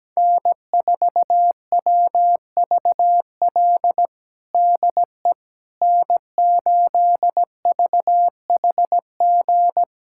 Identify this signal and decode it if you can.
Morse code, heard as a keyed tone, about 17 wpm, 700 Hz. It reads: N4WVL DE N8VHG
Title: USA Callsign DE Callsign - 5 Characters - 17wpm